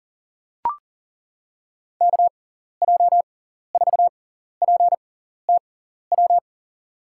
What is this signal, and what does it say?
Morse code, heard as a keyed tone, about 40 wpm, spaced as for 14 wpm, 690 Hz. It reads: KJ4PTW
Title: USA Call Signs - 6x Character Spacing - 40wpm